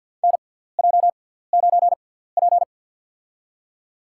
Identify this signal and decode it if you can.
Morse code, heard as a keyed tone, about 51 wpm, spaced as for 17 wpm, 700 Hz. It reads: NJ9P
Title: USA Call Signs - 6x Character Spacing - 50wpm